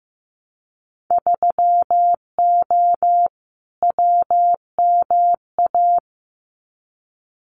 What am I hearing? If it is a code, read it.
Morse code, heard as a keyed tone, about 15 wpm, 700 Hz. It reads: 3O WMA
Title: Winter Field Day Exchanges Encore 15wpm